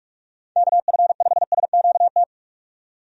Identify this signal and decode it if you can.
Morse code, heard as a keyed tone, about 42 wpm, 700 Hz. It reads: KF5SQT
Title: USA Call Signs - 6 Characters - 45wpm